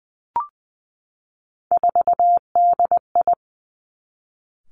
Morse code, heard as a keyed tone, about 19 wpm, 700 Hz. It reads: 4DI
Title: Three Challenging Letter-Number Combinations 20wpm